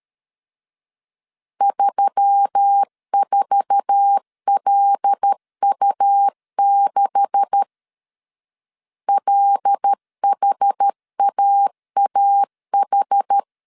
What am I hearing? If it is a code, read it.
Morse code, heard as a keyed tone, about 13 wpm, 800 Hz. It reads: 34LU6 LHAAH
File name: CW / DET / 1 9 1500C S B 12wpm 3819